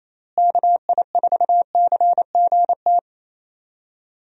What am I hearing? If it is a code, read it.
Morse code, heard as a keyed tone, about 28 wpm, 700 Hz. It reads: KI4CGT
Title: USA Call Signs - 6 Characters - Repeated 2x - 28wpm